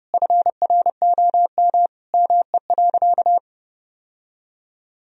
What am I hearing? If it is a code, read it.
Morse code, heard as a keyed tone, about 32 wpm, 700 Hz. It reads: FROM ME.
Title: Sentences from Top 1000 Words - Review Part 2 - 30wpm